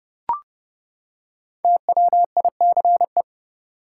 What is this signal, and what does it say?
Morse code, heard as a keyed tone, about 30 wpm, 700 Hz. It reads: TWICE
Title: Five Letter Words - 30wpm